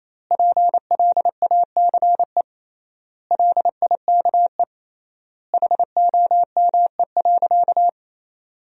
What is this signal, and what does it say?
Morse code, heard as a keyed tone, about 28 wpm, 700 Hz. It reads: PLACE LIKE HOME.
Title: 3x Word-Spacing - Sentences from Top 200 Words - 28wpm